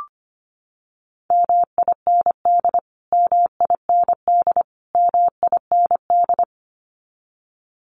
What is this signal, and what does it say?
Morse code, heard as a keyed tone, about 25 wpm, 700 Hz. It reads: MIND MIND MIND
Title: Top 500 Words Repeated 25wpm